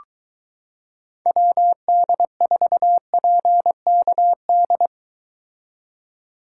Morse code, heard as a keyed tone, about 23 wpm, 700 Hz. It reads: WD4PKD